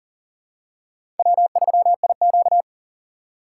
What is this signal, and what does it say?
Morse code, heard as a keyed tone, about 40 wpm, 700 Hz. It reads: W3IQ
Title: USA Call Signs - 4 Characters - 3x Character Spacing - 40wpm